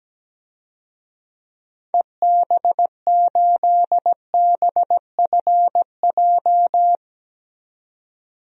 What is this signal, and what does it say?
Morse code, heard as a keyed tone, about 17 wpm, 700 Hz. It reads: EB8BFJ
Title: DXCC-100 Call Signs - 6 Characters - 17wpm